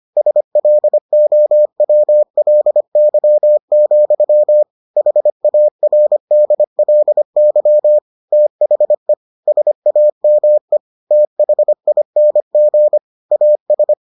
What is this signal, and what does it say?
Morse code, heard as a keyed tone, about 25 wpm, 600 Hz. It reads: SLOWLY, HARDLY THE SAME THING AS